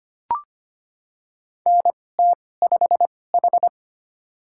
Morse code, heard as a keyed tone, about 25 wpm, spaced as for 18 wpm, 700 Hz. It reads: NT5H
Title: USA Call Signs - 4 Characters - 2x Character Spacing - 25wpm